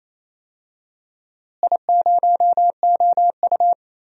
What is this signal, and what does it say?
Morse code, heard as a keyed tone, about 28 wpm, 700 Hz. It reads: I0OU
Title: DXCC-100 Call Signs - 4 Characters - Repeated 2x - 28wpm